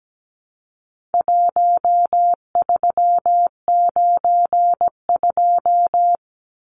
Morse code, heard as a keyed tone, about 17 wpm, 700 Hz. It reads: 1392
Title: Numbers - Contest Style - 4-Digit 17wpm